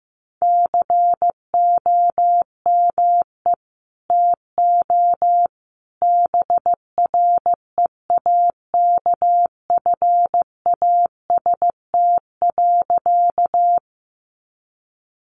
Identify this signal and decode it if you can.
Morse code, heard as a keyed tone, about 15 wpm, 700 Hz. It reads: COME TO BREAKFAST.
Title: Sentences from Top 1500 Words 15wpm